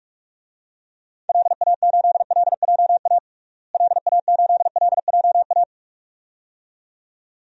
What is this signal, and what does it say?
Morse code, heard as a keyed tone, about 41 wpm, 700 Hz. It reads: LA8LJA LA8LJA
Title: DXCC-100 Call Signs - 6 Characters - Repeated 2x - 45wpm